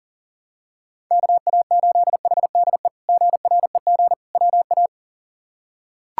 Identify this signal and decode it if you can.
Morse code, heard as a keyed tone, about 40 wpm, 700 Hz. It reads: KA8HDE GREG WA
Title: K1USN SST QSO Practice - 40wpm